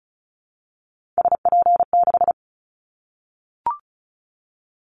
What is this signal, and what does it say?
Morse code, heard as a keyed tone, about 34 wpm, 700 Hz. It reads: SP6
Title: Callsign Prefixes for Top 25 DXCC - Repeated 3x - 35wpm